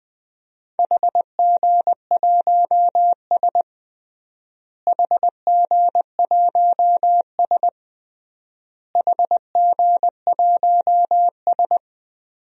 Morse code, heard as a keyed tone, about 20 wpm, 700 Hz. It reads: HG1S HG1S HG1S
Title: DXCC-100 Call Signs - 4 Characters - Repeated 3x - 20wpm